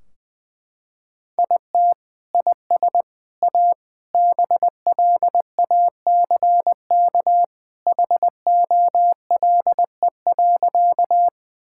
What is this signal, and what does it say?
Morse code, heard as a keyed tone, about 20 wpm, 700 Hz. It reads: IT IS A BLACK HOLE.
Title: Sentences from Top 1600 Words 20wpm